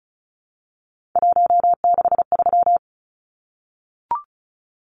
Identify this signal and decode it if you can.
Morse code, heard as a keyed tone, about 34 wpm, 700 Hz. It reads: 163
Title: Numbers - 3-Digit 35wpm